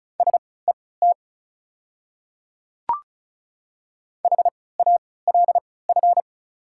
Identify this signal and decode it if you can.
Morse code, heard as a keyed tone, about 34 wpm, spaced as for 20 wpm, 700 Hz. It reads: SET HALF
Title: 3x Character Spacing - Top 500 Words - 35wpm